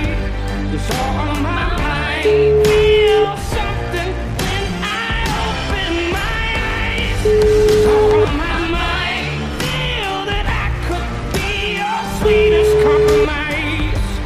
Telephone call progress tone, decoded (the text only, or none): ringback tone